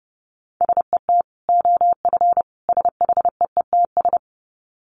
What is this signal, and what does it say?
Morse code, heard as a keyed tone, about 30 wpm, 700 Hz. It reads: SET OF SHEETS
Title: Dit Challenge 30wpm